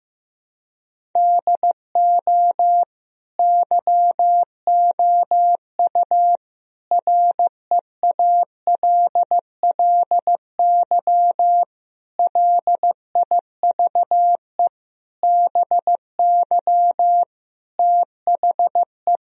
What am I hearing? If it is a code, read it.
Morse code, heard as a keyed tone, about 16 wpm, 700 Hz. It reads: DO YOU REALLY LIVE BY THE
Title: Sentences from Top 1000 Words - Review Part 1 - 15wpm